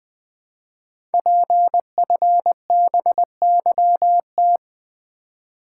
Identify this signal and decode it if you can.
Morse code, heard as a keyed tone, about 20 wpm, 700 Hz. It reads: PFBYT
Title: Code Groups - 5 Letters - 20wpm